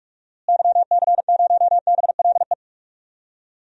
Morse code, heard as a keyed tone, about 46 wpm, 700 Hz. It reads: YC0BLE
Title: DXCC-100 Call Signs - 6 Characters - Repeated 3x - 45wpm